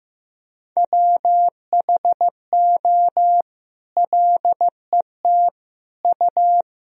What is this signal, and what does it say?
Morse code, heard as a keyed tone, about 15 wpm, 700 Hz. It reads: WHO LET U
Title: Sentences from Top 300 Words and Abbreviations 15wpm